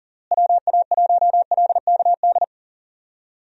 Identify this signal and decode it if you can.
Morse code, heard as a keyed tone, about 40 wpm, 700 Hz. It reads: WA1LKD
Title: USA Call Signs - 6 Characters - Repeated 3x - 40wpm